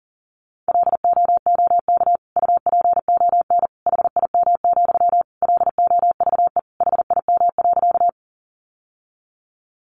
Morse code, heard as a keyed tone, about 40 wpm, 700 Hz. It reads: LOOK UPON HIM, LOVE HIM.
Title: Sentences from Top 1100 Words 40wpm